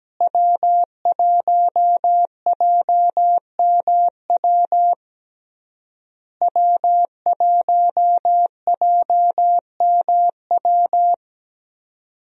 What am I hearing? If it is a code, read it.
Morse code, heard as a keyed tone, about 17 wpm, 700 Hz. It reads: W1JMW W1JMW
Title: USA Call Signs - 5 Characters - Repeated 3x - 17wpm